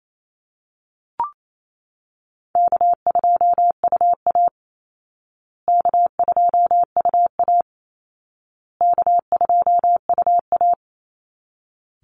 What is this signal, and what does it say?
Morse code, heard as a keyed tone, about 28 wpm, 700 Hz. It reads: K2UA K2UA K2UA
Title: Call Signs Repeated 2x - 28wpm